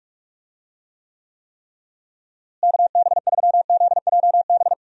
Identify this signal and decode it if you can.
Morse code, heard as a keyed tone, about 46 wpm, 700 Hz. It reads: KB3ZJB
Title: USA Call Signs - 6 Characters - Repeated 3x - 45wpm